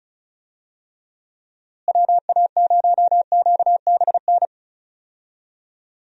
Morse code, heard as a keyed tone, about 36 wpm, 700 Hz. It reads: WA0QBN
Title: USA Call Signs - 6 Characters - Repeated 3x - 35wpm